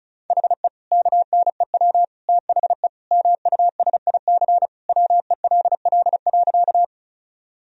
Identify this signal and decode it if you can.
Morse code, heard as a keyed tone, about 34 wpm, 700 Hz. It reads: HE KNEW THE MUSIC WELL.
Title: Sentences from Top 1000 Words - Review Part 1 - 35wpm